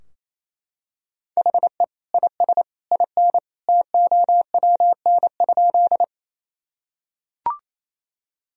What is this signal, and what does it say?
Morse code, heard as a keyed tone, about 28 wpm, 700 Hz. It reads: HE IS IN TOWN?
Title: Sentences from Top 1000 Words - Review Part 2 - 28wpm